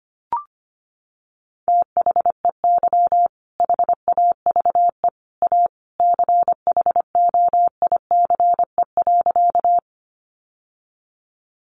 Morse code, heard as a keyed tone, about 25 wpm, 700 Hz. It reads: THEY HAVE A CHOICE.
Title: Sentences from Top 900 Words 25wpm